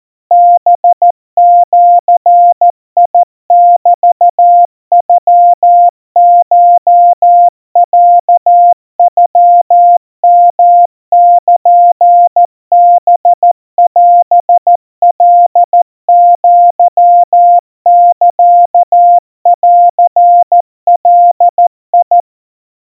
Morse code (Japanese Wabun code), heard as a keyed tone, about 14 wpm, 700 Hz. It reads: ハジメノコロノヨルハオカアサンガ
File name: kamyu 50 cpm 700 Hz Crispy